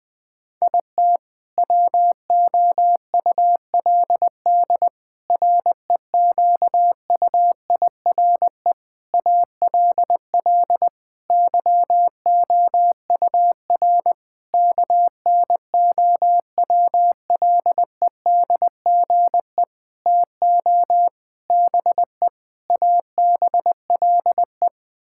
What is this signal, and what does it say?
Morse code, heard as a keyed tone, about 21 wpm, 700 Hz. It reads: IT WOULD REQUIRE ALL YOUR KNOWLEDGE TO BE ABLE
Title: Sentences from Top 1000 Words - Review Part 6 - 20wpm